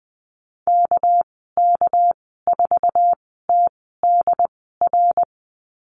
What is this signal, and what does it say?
Morse code, heard as a keyed tone, about 19 wpm, spaced as for 14 wpm, 700 Hz. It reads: KK4TDR
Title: 207 Lesson 33g - Letter j - Call Signs 20wpm